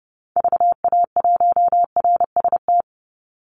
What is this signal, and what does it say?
Morse code, heard as a keyed tone, about 30 wpm, 700 Hz. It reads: VA1RST
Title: Call Signs Repeated 2x - 30wpm